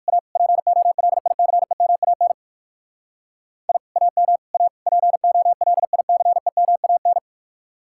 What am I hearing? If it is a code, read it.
Morse code, heard as a keyed tone, about 59 wpm, 700 Hz. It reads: A POLICEMAN I AM A POLICEMAN